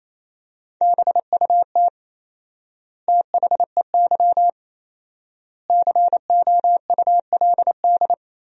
Morse code, heard as a keyed tone, about 28 wpm, 700 Hz. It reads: BUT THEY COULD
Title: 4x Word-Spacing - Sentences from Top 500 Words - 28wpm